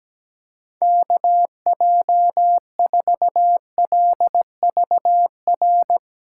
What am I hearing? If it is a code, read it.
Morse code, heard as a keyed tone, about 17 wpm, 700 Hz. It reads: KJ4LVR
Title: USA Call Signs - 6 Characters - 8x Character Spacing - 17wpm